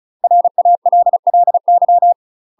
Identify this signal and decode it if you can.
Morse code, heard as a keyed tone, about 35 wpm, 700 Hz. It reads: RALLY
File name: UK MASTER QSOs TRAINING / 35WPM / UK MASTER QSOs 0014